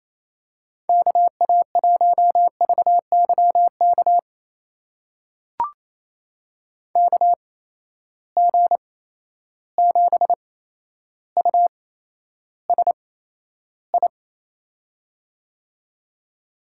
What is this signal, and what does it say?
Morse code, heard as a keyed tone, about 28 wpm, spaced as for 12 wpm, 700 Hz. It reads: KA1VYK KG7USI
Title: USA Call Signs - 6 Characters - 8x Character Spacing - 28wpm